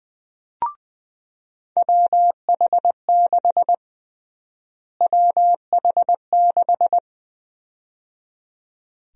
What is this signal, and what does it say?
Morse code, heard as a keyed tone, about 21 wpm, 700 Hz. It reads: WH6 WH6
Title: USA Call Sign Prefixes - Repeated 2x - 20wpm